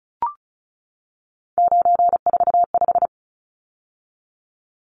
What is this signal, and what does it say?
Morse code, heard as a keyed tone, about 34 wpm, 700 Hz. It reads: 945
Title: Numbers - 3-Digit 35wpm